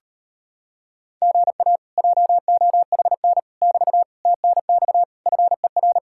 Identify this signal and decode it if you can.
Morse code, heard as a keyed tone, about 33 wpm, 700 Hz. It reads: GA JOHN <BT> TNX FER